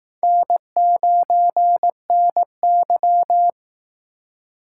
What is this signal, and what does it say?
Morse code, heard as a keyed tone, about 19 wpm, 700 Hz. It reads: N9NY